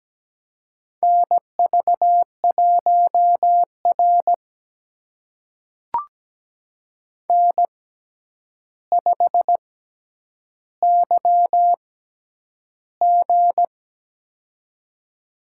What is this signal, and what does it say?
Morse code, heard as a keyed tone, about 17 wpm, spaced as for 9 wpm, 700 Hz. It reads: NV1R N5YG
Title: USA Call Signs - 4 Characters - 6x Character Spacing - 17wpm